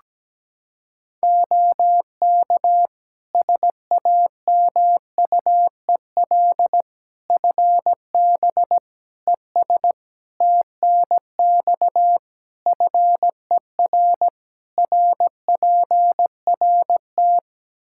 Morse code, heard as a keyed tone, about 17 wpm, 700 Hz. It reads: OK SAMUEL FB ES TNX FER RPRT
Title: K1USN SST QSO Practice 17wpm